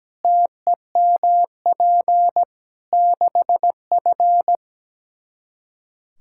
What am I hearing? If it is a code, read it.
Morse code, heard as a keyed tone, about 17 wpm, 700 Hz. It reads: TEMP 6F
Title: Practice QSO - Exchange 2 - Info - 17wpm